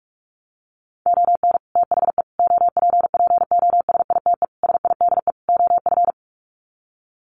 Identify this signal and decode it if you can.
Morse code, heard as a keyed tone, about 44 wpm, 700 Hz. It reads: ON THE OPPOSITE SIDE OF
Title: Sets of 5 Words Encore2 45wpm